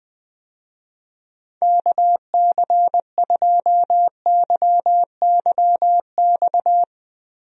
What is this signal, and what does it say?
Morse code, heard as a keyed tone, about 20 wpm, 700 Hz. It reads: KC2YYX
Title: USA Call Signs - 6 Characters - Repeated 2x - 20wpm